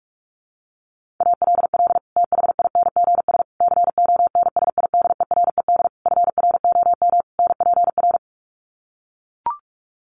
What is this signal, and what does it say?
Morse code, heard as a keyed tone, about 44 wpm, 700 Hz. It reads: ALL THINGS CONSIDERED FROM NPR
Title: Sets of 5 Words Encore 45wpm